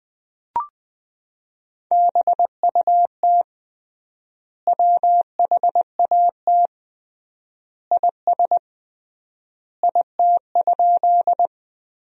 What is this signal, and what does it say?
Morse code, heard as a keyed tone, about 20 wpm, 700 Hz. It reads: BUT WHAT IS IT?
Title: 3x Word-Spacing - Sentences from Top 100 Words - 20wpm